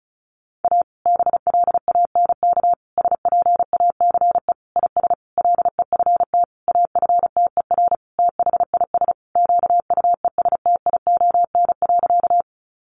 Morse code, heard as a keyed tone, about 34 wpm, 700 Hz. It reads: A BLANK SPACE IS LEFT AFTER THIS QUESTION.
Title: Sentences from Top 1500 Words 35wpm